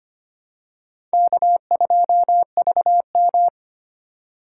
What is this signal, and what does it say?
Morse code, heard as a keyed tone, about 25 wpm, 700 Hz. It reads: K2VM
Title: USA Call Signs - 3x Character Spacing - 25wpm